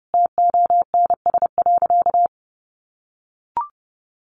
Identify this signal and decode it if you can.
Morse code, heard as a keyed tone, about 30 wpm, 700 Hz. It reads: TONS.
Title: A Sound Of Thunder 30wpm